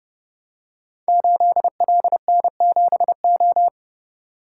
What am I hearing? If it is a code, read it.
Morse code, heard as a keyed tone, about 30 wpm, 700 Hz. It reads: 8LN7O
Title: Code Groups - 5 Letter:Numbers - 30wpm